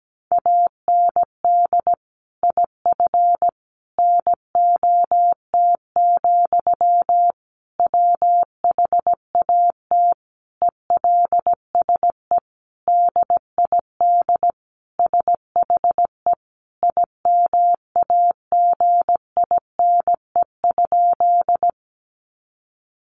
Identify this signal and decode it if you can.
Morse code, heard as a keyed tone, about 17 wpm, 700 Hz. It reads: AND IF NOT, WHAT ELSE DID SHE IMAGINE?
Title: Sentences from Top 1400 Words 17wpm